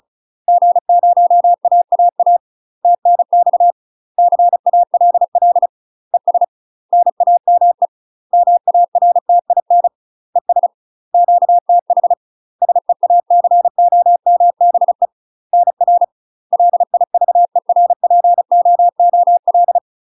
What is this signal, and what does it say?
Morse code, heard as a keyed tone, about 34 wpm, 700 Hz. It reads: G0AAA TNX CALL ES NAME MARTIN ES QTH SEACOMBE NR LIVERPOOL
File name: UK MASTER QSOs TRAINING / 35WPM / UK MASTER QSOs 0000